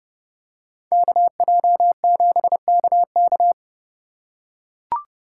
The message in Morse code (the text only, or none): KJ7KK